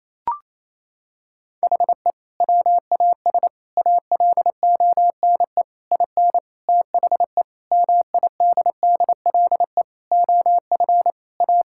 Morse code, heard as a keyed tone, about 28 wpm, 700 Hz. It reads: HE WAS ALONE IN THE MIDDLE OF A
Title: Sentences from Top 1000 Words - Review Part 6 - 28wpm